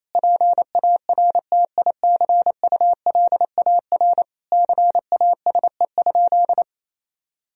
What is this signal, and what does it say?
Morse code, heard as a keyed tone, about 28 wpm, 700 Hz. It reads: PARTICULAR CASE?
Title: Sentences from Top 1000 Words - Review Part 4 - 28wpm